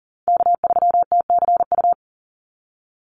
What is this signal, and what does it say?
Morse code, heard as a keyed tone, about 40 wpm, 700 Hz. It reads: K3TCU
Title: NA SOTA Chaser Callsigns 40wpm